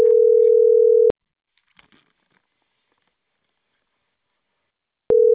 Ringback tone (North America), 440 + 480 Hz.